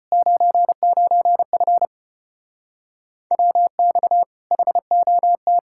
Morse code, heard as a keyed tone, about 33 wpm, 700 Hz. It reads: 99F WX HOT